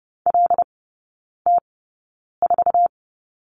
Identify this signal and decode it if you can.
Morse code, heard as a keyed tone, about 30 wpm, spaced as for 9 wpm, 690 Hz. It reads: LT4
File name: Lesson 17e - Letter u - Three Characters 30wpm